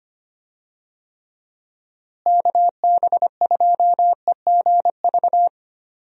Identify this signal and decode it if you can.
Morse code, heard as a keyed tone, about 25 wpm, 700 Hz. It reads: KB2EGV
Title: USA Call Signs - 6 Characters - Repeated 3x - 25wpm